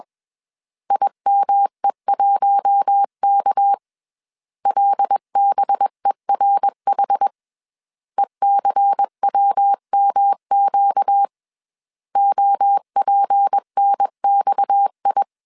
Morse code, heard as a keyed tone, about 21 wpm, 800 Hz. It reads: IME1K LBERH ECWMQ OPNXI
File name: CW / DET / 1 6 1500C S C 20wpm 2116